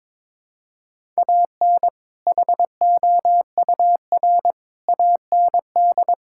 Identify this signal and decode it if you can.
Morse code, heard as a keyed tone, about 22 wpm, 700 Hz. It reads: AN HOUR AND
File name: practice / 22wpm / Build A Phrase - 5-Word Phrases - 22wpm